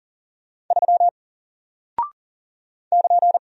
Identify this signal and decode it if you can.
Morse code, heard as a keyed tone, about 40 wpm, 700 Hz. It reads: 3 <KN>
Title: Single Character (Letter:Number:Punctuation:Prosign) Alt 40wpm